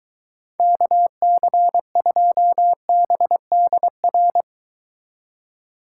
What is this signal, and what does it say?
Morse code, heard as a keyed tone, about 23 wpm, 700 Hz. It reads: KC2BDR